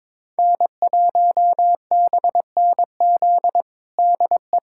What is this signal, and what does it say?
Morse code, heard as a keyed tone, about 22 wpm, 700 Hz. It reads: N1BNZ DE
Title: USA Callsign DE Callsign - 5 Characters - 22wpm